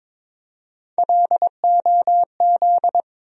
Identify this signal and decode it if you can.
Morse code, heard as a keyed tone, about 22 wpm, 700 Hz. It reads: LOZ
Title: USA Call Sign Postfixes - 3 Letters 22wpm